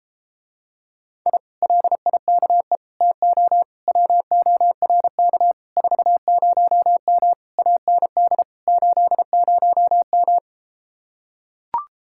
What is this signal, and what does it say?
Morse code, heard as a keyed tone, about 33 wpm, 700 Hz. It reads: I LIKE TO WORK 40M AND 80M